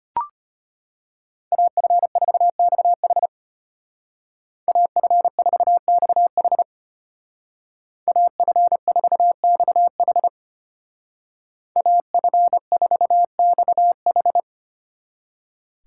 Morse code, heard as a keyed tone, about 30 wpm, 700 Hz. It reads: AF4XH AF4XH AF4XH AF4XH
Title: Speed Racing - USA 2x2 Call Signs 25wpm